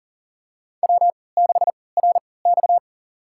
Morse code, heard as a keyed tone, about 38 wpm, spaced as for 22 wpm, 700 Hz. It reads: W6RX
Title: USA Call Signs - 3x Character Spacing - 40wpm